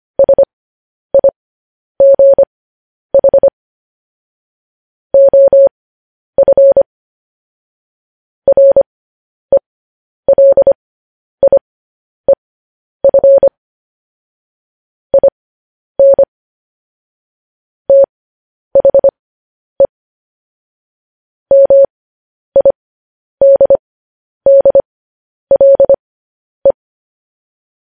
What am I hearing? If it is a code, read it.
Morse code, heard as a keyed tone, about 25 wpm, spaced as for 10 wpm, 550 Hz. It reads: SIGH OF RELIEF IN THE MIDDLE
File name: podcast / News headlines-1x-25x10